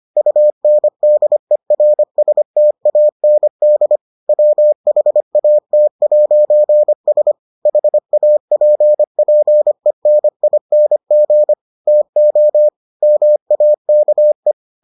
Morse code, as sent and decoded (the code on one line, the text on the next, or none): ..- -. -.. .-. ... - .- -. -.. / .-- .... .- - .----. ... / .... .- .--. .--. -. .. -. --. / - --- / -- .- -.- .
UNDERSTAND WHAT'S HAPPENING TO MAKE